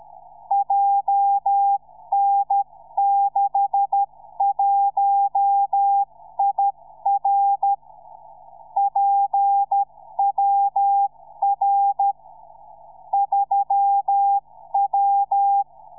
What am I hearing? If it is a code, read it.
Morse code, heard as a keyed tone, about 13 wpm, 800 Hz. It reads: JN61IR PWR 3W
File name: iw0ffk / 2007 04 22 iw0ffk beacon